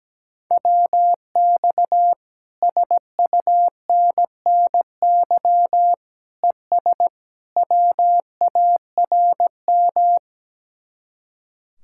Morse code, heard as a keyed tone, about 17 wpm, 700 Hz. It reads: WX SUNNY ES WARM
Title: Speed-Racing Common QSO Phrases Encore - 17wpm